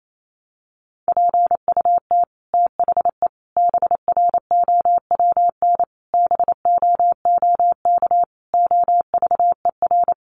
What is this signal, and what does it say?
Morse code, heard as a keyed tone, about 28 wpm, 700 Hz. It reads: PUT THE BROWN BOOK OVER
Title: Sentences from Top 1200 Words 28wpm